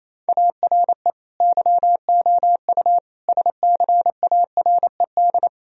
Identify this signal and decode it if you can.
Morse code, heard as a keyed tone, about 28 wpm, 700 Hz. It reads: ARE YOU SCARED
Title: Sentences from Top 1000 Words - Review Part 4 - 28wpm